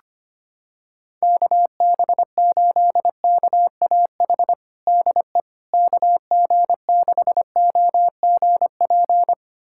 Morse code, heard as a keyed tone, about 25 wpm, 700 Hz. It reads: KB8KAH DE KG6OGP
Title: USA Callsign DE Callsign - 6 Characters - 25wpm